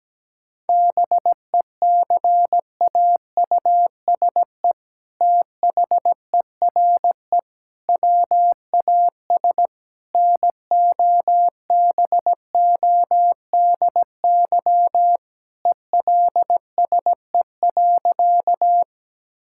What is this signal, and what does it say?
Morse code, heard as a keyed tone, about 17 wpm, 700 Hz. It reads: BECAUSE THERE WAS NOBODY ELSE.
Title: Sentences from Top 1600 Words 17wpm